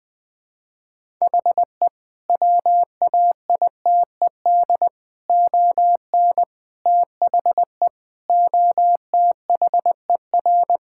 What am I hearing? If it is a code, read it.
Morse code, heard as a keyed tone, about 20 wpm, 700 Hz. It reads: HE WAITED ON THE OTHER
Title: Sentences with 4-Word Phrases - 20wpm